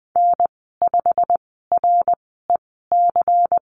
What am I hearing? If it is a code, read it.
Morse code, heard as a keyed tone, about 20 wpm, spaced as for 14 wpm, 700 Hz. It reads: N5REC
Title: 219 Lesson 35g - Number 0 - Call Signs 20wpm